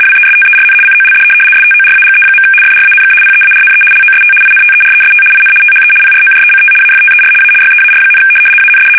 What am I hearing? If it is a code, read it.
RTTY, about 75 baud. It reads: WESTERN COLD STORAGE. FRANK PEARNE LOST INTEREST IN THE PROJECT AFTER A YEAR, AND
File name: fsk850-75B / teletype-corp